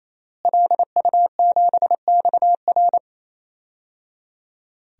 Morse code, heard as a keyed tone, about 28 wpm, 700 Hz. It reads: LU7XR
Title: DXCC-100 Call Signs - 5 Characters - Repeated 3x - 28wpm